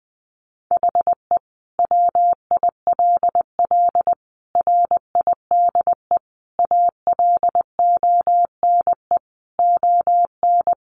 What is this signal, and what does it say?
Morse code, heard as a keyed tone, about 20 wpm, 700 Hz. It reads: HE WILL RIDE ALONE ON
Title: Sentences from Top 1400 Words 20wpm